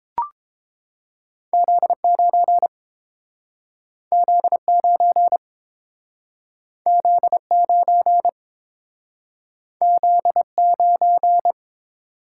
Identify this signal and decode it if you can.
Morse code, heard as a keyed tone, about 30 wpm, 700 Hz. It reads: Z9 Z9 Z9 Z9